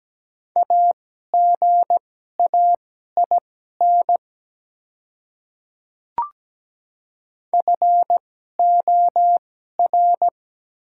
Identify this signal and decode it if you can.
Morse code, heard as a keyed tone, about 17 wpm, spaced as for 12 wpm, 700 Hz. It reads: AGAIN FOR
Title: 2x Character Spacing - Top 300 Words - 17wpm